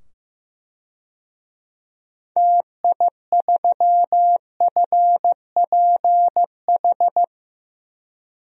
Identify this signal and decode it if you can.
Morse code, heard as a keyed tone, about 15 wpm, 700 Hz. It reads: TI3FPH
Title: DXCC-100 Call Signs - 6 Characters - 15wpm